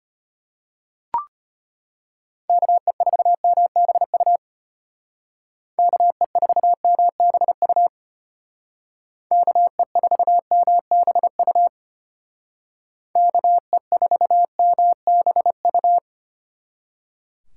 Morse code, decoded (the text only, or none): KE4MBU KE4MBU KE4MBU KE4MBU